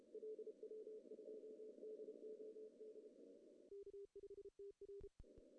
Morse code, heard as a keyed tone, about 30 wpm, 410 Hz. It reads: LW2DOD K5TR
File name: CW / cqww-cw-k5tr